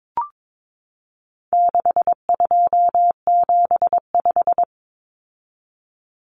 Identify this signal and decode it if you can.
Morse code, heard as a keyed tone, about 22 wpm, 700 Hz. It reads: 6275
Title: Common Cut Numbers - 4-Digit 22wpm